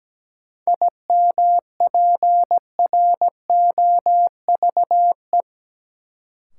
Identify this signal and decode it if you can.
Morse code, heard as a keyed tone, about 17 wpm, 700 Hz. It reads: IMPROVE
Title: Sentences from Top 800 Words - New Word Drill - 17wpm